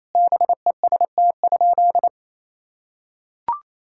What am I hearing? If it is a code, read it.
Morse code, heard as a keyed tone, about 27 wpm, 700 Hz. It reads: BEST?
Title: Sentences from Top 1000 Words - Review Part 5 - 28wpm